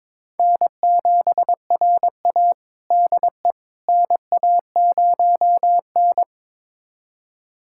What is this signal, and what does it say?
Morse code, heard as a keyed tone, about 22 wpm, 700 Hz. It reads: N7RA DE NA0N
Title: USA Callsign DE Callsign - 4 Characters - 22wpm